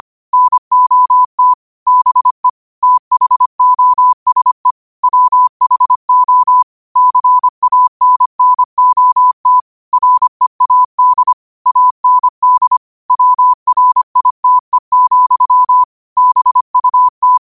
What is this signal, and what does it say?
Morse code, heard as a keyed tone, about 25 wpm, 1000 Hz. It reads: NOT BE THOSE WHO CANNOT READ AND WRITE, BUT
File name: qotdfiles / 25wpm